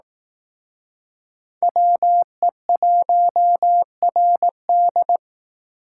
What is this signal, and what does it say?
Morse code, heard as a keyed tone, about 18 wpm, 700 Hz. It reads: WE1RD
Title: Speed Racing - USA 2x2 Call Signs 15wpm